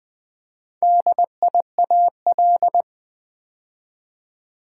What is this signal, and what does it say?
Morse code, heard as a keyed tone, about 20 wpm, 700 Hz. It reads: DIAL
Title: Build A Word - 2 Letters At A Time - 8-Letter Words - 20wpm